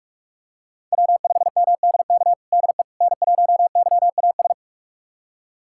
Morse code, heard as a keyed tone, about 46 wpm, 700 Hz. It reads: W5MDK DE N1YAS
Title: USA Callsign DE Callsign - 5 Characters - 45wpm